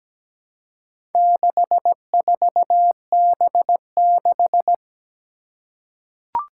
Morse code, heard as a keyed tone, about 17 wpm, 700 Hz. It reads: 64B6
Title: Four Challenging Letter-Number Combinations 17wpm